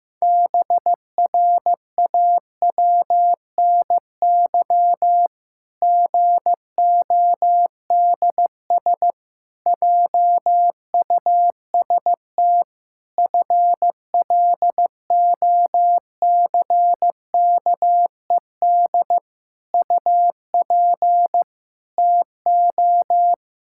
Morse code, heard as a keyed tone, about 15 wpm, 700 Hz. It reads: BRAWNY GODS JUST FLOCKED UP TO
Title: Pangram Challenge v2 - 15wpm